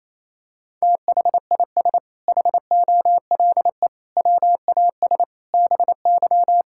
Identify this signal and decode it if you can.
Morse code, heard as a keyed tone, about 28 wpm, 700 Hz. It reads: THIS HOLE WAS BY
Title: Sentences from Top 1600 Words 28wpm